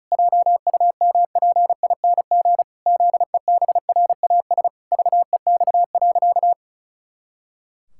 Morse code, heard as a keyed tone, about 34 wpm, 700 Hz. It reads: JUMPING ZEBRAS VEX.
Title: Pangram Challenge v2 - 35wpm